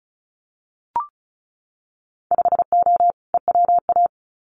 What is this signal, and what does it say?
Morse code, heard as a keyed tone, about 34 wpm, 700 Hz. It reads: 5O EWA
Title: Winter Field Day Exchanges Encore 35wpm